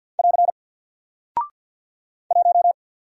Morse code, heard as a keyed tone, about 48 wpm, 700 Hz. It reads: <AR> 1
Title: Single Character (Letter:Number:Punctuation:Prosign) Alt 50wpm